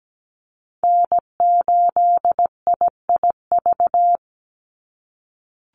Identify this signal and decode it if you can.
Morse code, heard as a keyed tone, about 17 wpm, 700 Hz. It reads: N8IIV
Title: USA Call Signs 17wpm